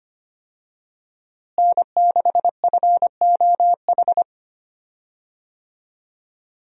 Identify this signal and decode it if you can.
Morse code, heard as a keyed tone, about 25 wpm, 700 Hz. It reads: N6FOH
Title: USA Call Signs - 5 Characters - Repeated 3x - 25wpm